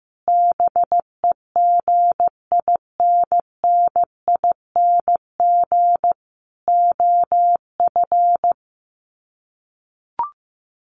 Morse code, heard as a keyed tone, about 15 wpm, 700 Hz. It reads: BEGINNING OF
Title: Sets of 4 Words 15wpm